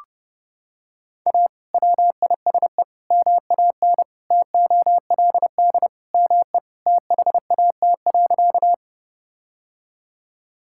Morse code, heard as a keyed tone, about 30 wpm, 700 Hz. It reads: A WISE MAN TOLD ME THAT.
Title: Sentences from Top 1600 Words 30wpm